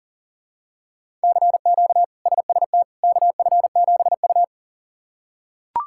Morse code, heard as a keyed tone, about 40 wpm, 700 Hz. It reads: CQ SST KF7U
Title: K1USN SST QSO Practice - 40wpm